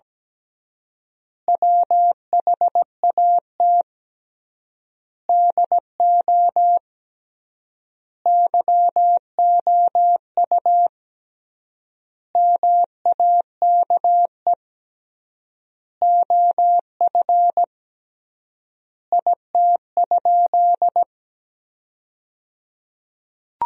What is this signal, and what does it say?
Morse code, heard as a keyed tone, about 17 wpm, 700 Hz. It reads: WHAT DO YOU MAKE OF IT?
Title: 3x Word-Spacing - Sentences from Top 100 Words - 17wpm